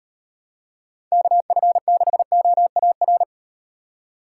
Morse code, heard as a keyed tone, about 38 wpm, 700 Hz. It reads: KF6OAR